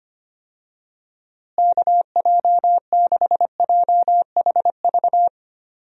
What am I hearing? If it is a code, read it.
Morse code, heard as a keyed tone, about 25 wpm, 700 Hz. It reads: KJ6JHV
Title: USA Call Signs - 6 Characters - Repeated 2x - 25wpm